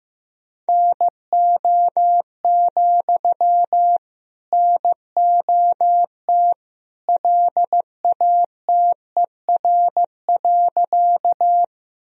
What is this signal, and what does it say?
Morse code, heard as a keyed tone, about 15 wpm, 700 Hz. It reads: NO, NOT LATER.
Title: Sentences from Top 1000 Words - Review Part 4 - 15wpm